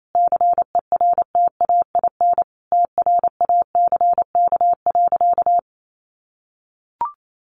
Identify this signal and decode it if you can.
Morse code, heard as a keyed tone, about 28 wpm, 700 Hz. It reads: CERTAIN TRACK.
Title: Sentences from Top 1100 Words 28wpm